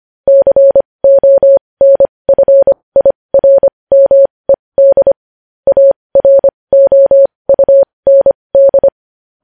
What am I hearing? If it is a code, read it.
Morse code, heard as a keyed tone, about 25 wpm, spaced as for 20 wpm, 550 Hz. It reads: CONFIRMED AROUND
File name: podcast / News headlines-1x-25x20